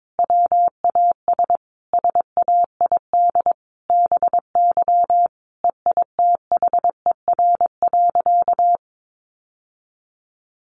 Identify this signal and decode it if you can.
Morse code, heard as a keyed tone, about 22 wpm, 700 Hz. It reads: WAS SAID BY EITHER.
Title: Sentences from Top 600 Words 22wpm